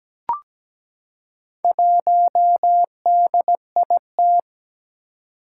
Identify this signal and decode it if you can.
Morse code, heard as a keyed tone, about 17 wpm, 700 Hz. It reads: 1DIT
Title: Four Challenging Letter-Number Combinations 17wpm